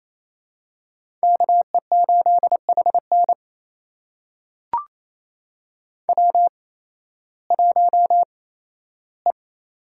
Morse code, heard as a keyed tone, about 28 wpm, spaced as for 15 wpm, 700 Hz. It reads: KE8HN W1E
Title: USA Call Signs - 5 Characters - 8x Character Spacing - 28wpm